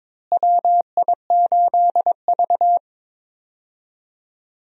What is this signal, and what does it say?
Morse code, heard as a keyed tone, about 22 wpm, 700 Hz. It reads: WI8V
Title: USA Call Signs - 4 Characters - Repeated 3x - 22wpm